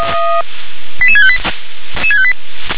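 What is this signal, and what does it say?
Five-tone selective call (ZVEI), heard as quick runs of five tones, tones of about 70 ms. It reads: 88658
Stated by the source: ZVEI five-tone call received by SDR.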